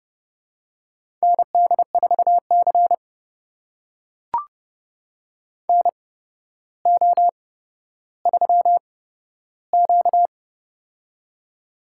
Morse code, heard as a keyed tone, about 30 wpm, spaced as for 13 wpm, 700 Hz. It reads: ND4C NO3Q
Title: USA Call Signs - 4 Characters - 8x Character Spacing - 30wpm